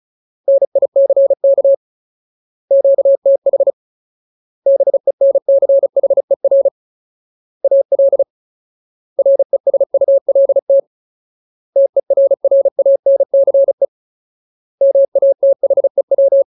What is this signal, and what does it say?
Morse code, heard as a keyed tone, about 34 wpm, 550 Hz. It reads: NICK QTH BENCHER AL RESULT TERRANCE MATHEW
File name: Ultimate Word List / 35WPM Ultimate